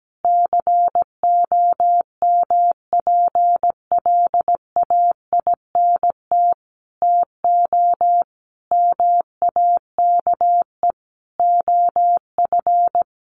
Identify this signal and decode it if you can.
Morse code, heard as a keyed tone, about 17 wpm, 700 Hz. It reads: COMPLAINT TO MAKE OF
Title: Sentences from Top 1400 Words 17wpm